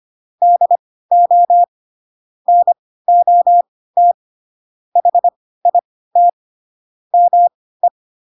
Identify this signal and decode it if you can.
Morse code, heard as a keyed tone, about 25 wpm, spaced as for 16 wpm, 700 Hz. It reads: DO NOT HIT ME
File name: Pinocchio / Pinocchio ch1 10 25 ik1hge